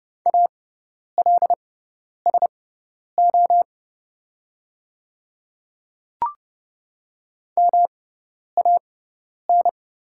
Morse code, heard as a keyed tone, about 30 wpm, spaced as for 10 wpm, 700 Hz. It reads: ALSO MAN
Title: 6x Character Spacing - Top 200 Words - 30wpm